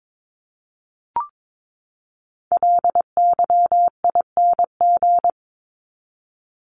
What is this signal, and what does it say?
Morse code, heard as a keyed tone, about 22 wpm, 700 Hz. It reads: LYING
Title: ING Suffix - Top 100 Words with Base Words 22wpm